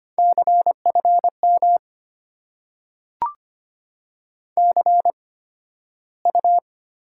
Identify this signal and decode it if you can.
Morse code, heard as a keyed tone, about 25 wpm, spaced as for 13 wpm, 700 Hz. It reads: CFM CU
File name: Top 90 QSO Elements - 8x Character Spacing - 25wpm